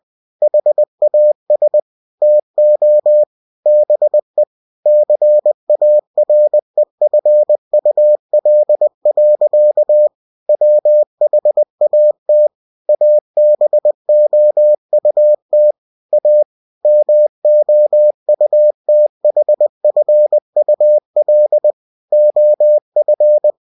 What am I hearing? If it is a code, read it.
Morse code, heard as a keyed tone, about 20 wpm, 600 Hz. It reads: HAS TO BE CAREFUL. WHAT ABOUT A MOUTHFUL OF
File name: cw-books / winnie-the-pooh / winnie-20wpm-page-0015